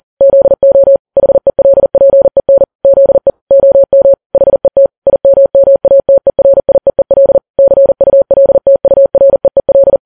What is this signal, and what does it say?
Morse code, heard as a keyed tone, about 40 wpm, 550 Hz. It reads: ZO HELPEN ZE OM HET IMMATERIEEL CULTUREEL
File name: Files Web tls / Morse on9fv / morse40kps